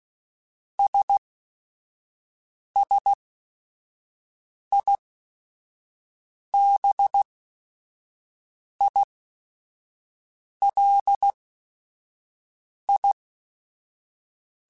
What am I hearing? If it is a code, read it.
Morse code, heard as a keyed tone, about 16 wpm, spaced as for 5 wpm, 800 Hz. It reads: SSIBILI
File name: Technical Subjects / Technical Subjects-mp3s / AND Gate @5